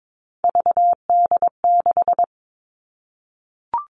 Morse code, heard as a keyed tone, about 22 wpm, 700 Hz. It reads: VD6
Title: Three Challenging Letter-Number Combinations 22wpm